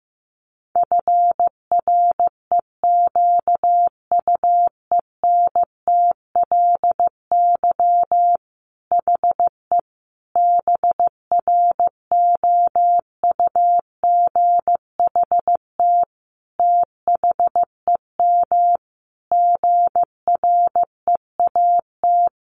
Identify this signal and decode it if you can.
Morse code, heard as a keyed tone, about 15 wpm, 700 Hz. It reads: FREQUENTLY HE BROUGHT THEM GREAT
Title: Sentences from Top 1300 Words 15wpm